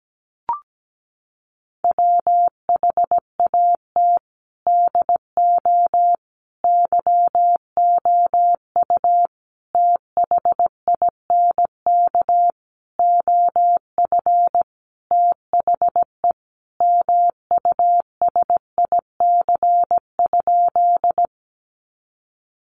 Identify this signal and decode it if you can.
Morse code, heard as a keyed tone, about 17 wpm, 700 Hz. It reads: WHAT DO YOU THINK OF THE MUSIC?
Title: Sentences from Top 300 Words 17wpm